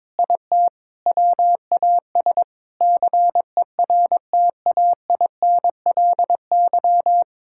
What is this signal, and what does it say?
Morse code, heard as a keyed tone, about 22 wpm, 700 Hz. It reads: IT WAS CERTAINLY
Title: Sentences from Top 1000 Words - Review Part 6 - 22wpm